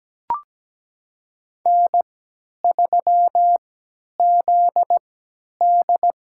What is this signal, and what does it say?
Morse code, heard as a keyed tone, about 17 wpm, spaced as for 10 wpm, 700 Hz. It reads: N3ZD
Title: USA Call Signs - 4 Characters - 3x Character Spacing - 17wpm